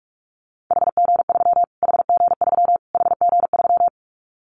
Morse code, heard as a keyed tone, about 44 wpm, 700 Hz. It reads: HG3 HG3 HG3
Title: Callsign Prefixes for Top 25 DXCC - Repeated 3x - 45wpm